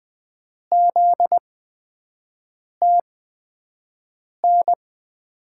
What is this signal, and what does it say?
Morse code, heard as a keyed tone, about 20 wpm, spaced as for 5 wpm, 700 Hz. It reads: ZTN
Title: USA Call Signs - 6 Characters - 8x Character Spacing - 20wpm